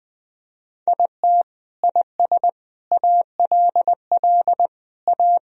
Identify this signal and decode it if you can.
Morse code, heard as a keyed tone, about 19 wpm, 700 Hz. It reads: IT IS ALL A
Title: Sentences from Top 1000 Words - Review Part 3 - 20wpm